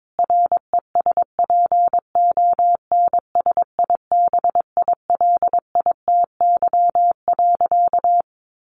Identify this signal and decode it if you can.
Morse code, heard as a keyed tone, about 22 wpm, 700 Hz. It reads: RESPONSIBILITY.
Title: Sentences from Top 1500 Words 22wpm